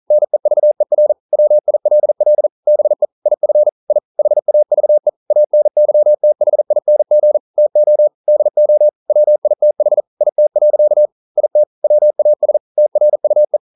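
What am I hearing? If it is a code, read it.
Morse code, heard as a keyed tone, about 39 wpm, 600 Hz. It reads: NEVER WILL BE IF I HAVE ANYTHING TO DO WITH IT. IT WAS TRUE
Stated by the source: Morse code practice at 41 wpm, from CW med Gnister.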